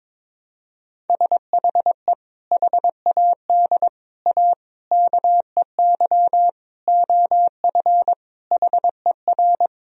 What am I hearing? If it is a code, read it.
Morse code, heard as a keyed tone, about 22 wpm, 700 Hz. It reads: SHE HAD A KEY OF HER
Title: Sentences from Top 1000 Words - Review Part 4 - 22wpm